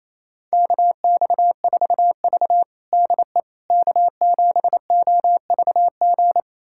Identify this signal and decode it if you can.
Morse code, heard as a keyed tone, about 28 wpm, 700 Hz. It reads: KX4V DE K7OVG
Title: USA Callsign DE Callsign 28wpm